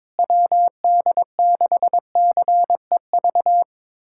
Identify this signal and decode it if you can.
Morse code, heard as a keyed tone, about 22 wpm, 700 Hz. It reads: WD6CEV
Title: USA Callsign DE Callsign - 6 Characters - 22wpm